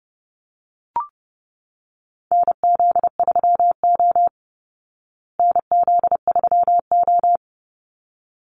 Morse code, heard as a keyed tone, about 30 wpm, 700 Hz. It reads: NZ3O NZ3O
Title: Call Signs Repeated 3x - 30wpm